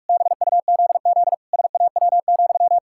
Morse code, heard as a keyed tone, about 41 wpm, 700 Hz. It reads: BUZZ SAW,
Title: Pangram Challenge v2 - 45wpm